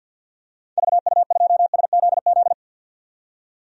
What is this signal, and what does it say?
Morse code, heard as a keyed tone, about 52 wpm, 700 Hz. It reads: VU2SZZ
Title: DXCC-100 Call Signs - 6 Characters - 50wpm